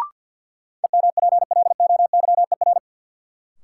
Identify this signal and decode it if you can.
Morse code, heard as a keyed tone, about 52 wpm, 700 Hz. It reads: EMPLOYER
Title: Sentences from Top 1600 Words - New Word Drill Alt - 50wpm